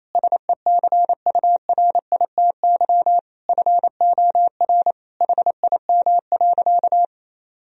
Morse code, heard as a keyed tone, about 27 wpm, 700 Hz. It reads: SECURITY FOR HIM.
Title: Sentences from Top 1000 Words - Review Part 4 - 28wpm